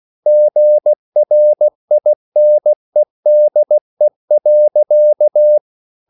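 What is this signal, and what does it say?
Morse code, heard as a keyed tone, about 16 wpm, 600 Hz. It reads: GRINEDE.
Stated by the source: Slow (QRS) Morse code at 16 wpm, from CW med Gnister.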